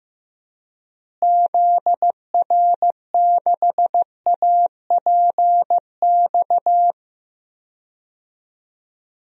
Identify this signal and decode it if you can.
Morse code, heard as a keyed tone, about 16 wpm, 700 Hz. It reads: ZR6APX
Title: DXCC-100 Call Signs - 6 Characters - Repeated 2x - 15wpm